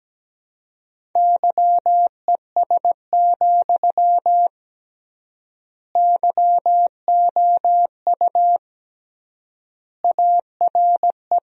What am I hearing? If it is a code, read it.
Morse code, heard as a keyed tone, about 17 wpm, 700 Hz. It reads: YES, YOU ARE
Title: 3x Word-Spacing - Sentences from Top 500 Words - 17wpm